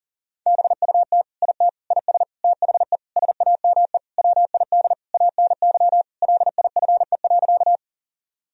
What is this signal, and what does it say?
Morse code, heard as a keyed tone, about 40 wpm, 700 Hz. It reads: BUT IT IS THE SAME WID ANY LIFE.
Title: Sentences from Top 300 Words and Abbreviations 40wpm